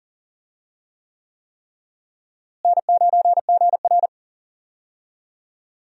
Morse code, heard as a keyed tone, about 40 wpm, 700 Hz. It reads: N9GR
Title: USA Call Signs - 4 Characters - Repeated 3x - 40wpm